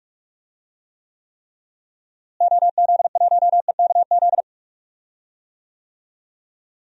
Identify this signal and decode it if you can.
Morse code, heard as a keyed tone, about 43 wpm, 700 Hz. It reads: OZ1EKZ
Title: DXCC-100 Call Signs - 6 Characters - Repeated 3x - 45wpm